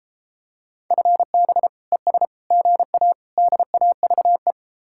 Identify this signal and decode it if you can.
Morse code, heard as a keyed tone, about 33 wpm, 700 Hz. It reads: FB ES GA DAVE